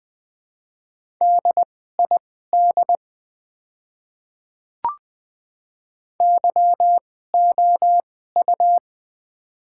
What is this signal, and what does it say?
Morse code, heard as a keyed tone, about 20 wpm, spaced as for 14 wpm, 700 Hz. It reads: DID YOU
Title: 2x Character Spacing - Top 100 Words - 20wpm